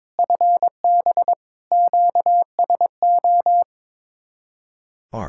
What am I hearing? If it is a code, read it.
Morse code, heard as a keyed tone, about 22 wpm, 700 Hz. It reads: FB QSO
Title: Speed-Racing Common QSO Phrases Encore - 22wpm